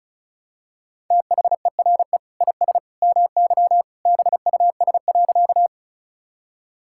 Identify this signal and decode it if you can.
Morse code, heard as a keyed tone, about 34 wpm, 700 Hz. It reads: THERE IS MY BUS.
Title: Sentences from Top 1000 Words - Review Part 6 - 35wpm